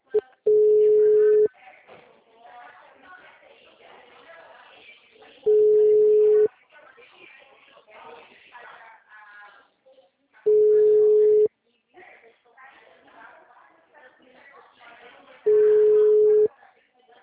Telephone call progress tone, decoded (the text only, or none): ringback tone